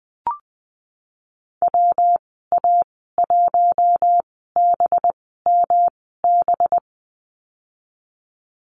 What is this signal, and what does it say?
Morse code, heard as a keyed tone, about 19 wpm, spaced as for 14 wpm, 700 Hz. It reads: WA1BMB
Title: 213 Lesson 34g - Number 8 - Call Signs 20wpm